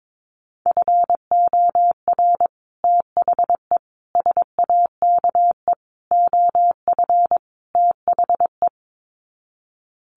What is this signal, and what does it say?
Morse code, heard as a keyed tone, about 22 wpm, 700 Hz. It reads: FOR THE SAKE OF THE
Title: Sets of 5 Words Encore 22wpm